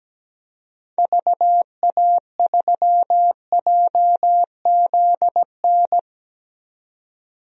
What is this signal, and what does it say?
Morse code, heard as a keyed tone, about 17 wpm, 700 Hz. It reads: VA3JZN
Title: DXCC-100 Call Signs - 6 Characters - 17wpm